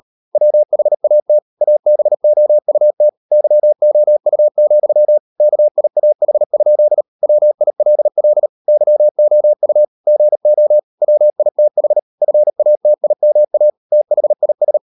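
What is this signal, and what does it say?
Morse code, heard as a keyed tone, about 38 wpm, 600 Hz. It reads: WHAT ABOUT YOU, KIAH? WILL YOU GO WITH FATIMA THIS
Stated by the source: Morse code practice at 38 WPM, sending Ken Follett's novel Never in English, from CW med Gnister.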